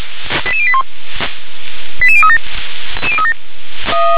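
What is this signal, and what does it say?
Five-tone selective call (ZVEI), heard as quick runs of five tones, tones of about 70 ms. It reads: 88081 88037
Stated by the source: ZVEI five-tone selective call received by SDR.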